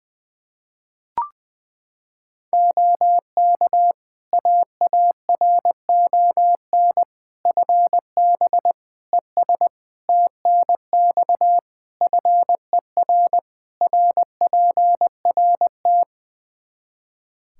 Morse code, heard as a keyed tone, about 20 wpm, 700 Hz. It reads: OK AARON FB ES TNX FER RPRT
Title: Practice QSO - Exchange 2 - Info - 20wpm